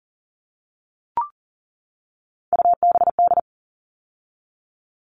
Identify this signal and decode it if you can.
Morse code, heard as a keyed tone, about 40 wpm, 700 Hz. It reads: UBD
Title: All Cut Numbers - 3-Digit 40wpm